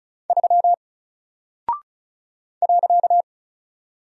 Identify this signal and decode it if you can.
Morse code, heard as a keyed tone, about 35 wpm, 700 Hz. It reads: 3 .
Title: Single Character (Letter:Number:Punctuation:Prosign) 35wpm